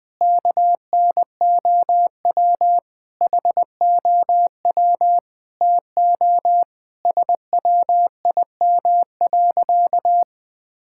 Morse code, heard as a keyed tone, about 20 wpm, 700 Hz. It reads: KNOW HOW TO SWIM.
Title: Sentences with 4-Word Phrases - 20wpm